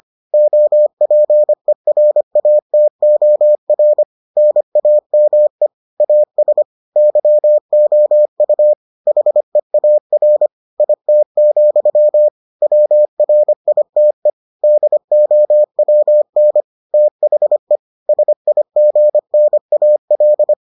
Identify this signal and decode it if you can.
Morse code, heard as a keyed tone, about 25 wpm, 600 Hz. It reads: OPERATOR NAME AS YOU HEAR IT, WRITE DOWN THE SIGNAL